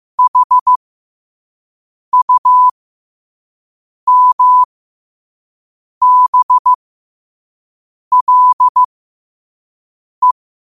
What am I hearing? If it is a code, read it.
Morse code, heard as a keyed tone, about 15 wpm, spaced as for 5 wpm, 1000 Hz. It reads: HUMBLE
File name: qotdfiles / 5wpm